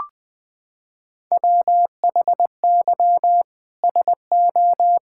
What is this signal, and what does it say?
Morse code, heard as a keyed tone, about 20 wpm, 700 Hz. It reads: WHY SO
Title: Sentences from Top 1000 Words - Review Part 5 - 20wpm